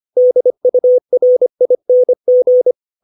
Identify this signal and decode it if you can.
Morse code, heard as a keyed tone, about 25 wpm, 500 Hz. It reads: DURING
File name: 632 words / 632 Words 25WPM